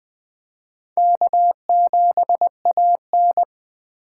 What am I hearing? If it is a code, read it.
Morse code, heard as a keyed tone, about 20 wpm, 700 Hz. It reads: K7AN
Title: USA Call Signs - 4 Characters - 4x Character Spacing - 20wpm